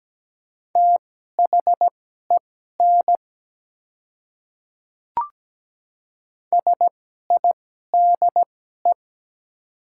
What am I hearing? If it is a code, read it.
Morse code, heard as a keyed tone, about 17 wpm, spaced as for 12 wpm, 690 Hz. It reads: THEN SIDE
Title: 2x Character Spacing - Top 100 Words - 17wpm